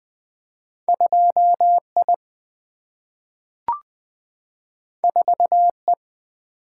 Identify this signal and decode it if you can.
Morse code, heard as a keyed tone, about 20 wpm, 700 Hz. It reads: 2I 4E
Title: Single-Letter Single-Number 20wpm